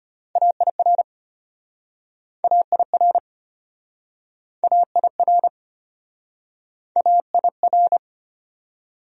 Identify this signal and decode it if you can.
Morse code, heard as a keyed tone, about 34 wpm, 700 Hz. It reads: AIR AIR AIR AIR